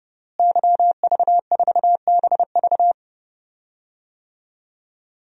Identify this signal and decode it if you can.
Morse code, heard as a keyed tone, about 30 wpm, 700 Hz. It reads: YV4BV
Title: DXCC-100 Call Signs - 5 Characters - Repeated 3x - 30wpm